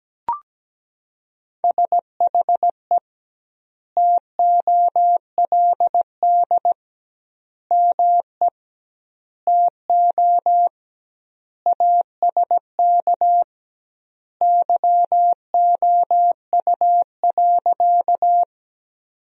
Morse code, heard as a keyed tone, about 17 wpm, 700 Hz. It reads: SHE TOLD ME TO ASK YOU.
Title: 2x Word-Spacing - Sentences from Top 400 Words - 17wpm